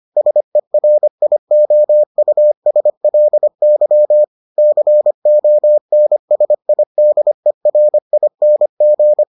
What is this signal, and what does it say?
Morse code, heard as a keyed tone, about 25 wpm, 600 Hz. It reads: SERIOUSLY CONSIDERING